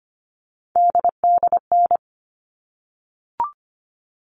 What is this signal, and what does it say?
Morse code, heard as a keyed tone, about 25 wpm, 700 Hz. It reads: DDN
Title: All Cut Numbers - 3-Digit 25wpm